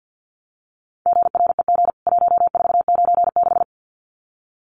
Morse code, heard as a keyed tone, about 50 wpm, 700 Hz. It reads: GREG 1496
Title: CWT Exchange 50wpm